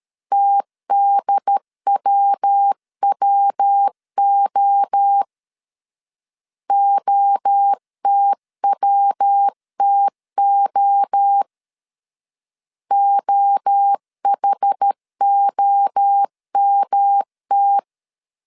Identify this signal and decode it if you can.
Morse code, heard as a keyed tone, about 13 wpm, 800 Hz. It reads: TDWWO OTWTO OHOMT
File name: CW / DET / 1 3 1500C S A 12wpm 2905